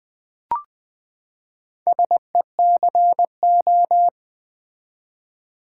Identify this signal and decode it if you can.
Morse code, heard as a keyed tone, about 20 wpm, 700 Hz. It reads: SECO
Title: Build A Word - 4 Letters At A Time - 8-Letter Words - 20wpm